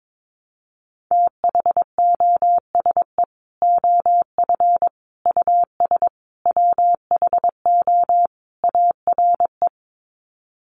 Morse code, heard as a keyed tone, about 22 wpm, 700 Hz. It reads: THOSE OF US WHO ARE
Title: Sets of 5 Words Encore2 22wpm